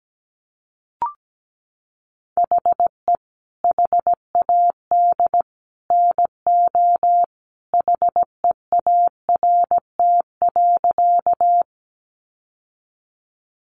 Morse code, heard as a keyed tone, about 17 wpm, 700 Hz. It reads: HE HAD NO HEART.
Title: Sentences from Top 1200 Words 17wpm